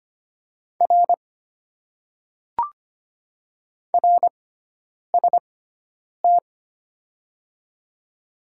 Morse code, heard as a keyed tone, about 25 wpm, spaced as for 9 wpm, 700 Hz. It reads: R RST
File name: Top 90 QSO Elements - 6x Character Spacing - 25wpm